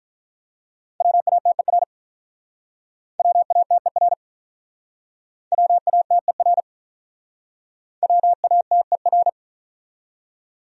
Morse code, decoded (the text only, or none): WATER WATER WATER WATER